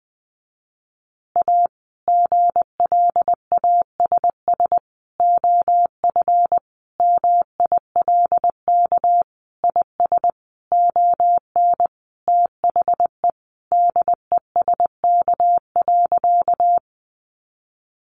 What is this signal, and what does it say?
Morse code, heard as a keyed tone, about 20 wpm, 700 Hz. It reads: A GLASS OF MILK IS ON THE DESK.
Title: Additional Easy Phrases 20wpm